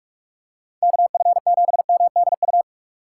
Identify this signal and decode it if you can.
Morse code, heard as a keyed tone, about 44 wpm, 700 Hz. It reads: KF7MDU
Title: USA Call Signs - 6 Characters - 2x Character Spacing - 45wpm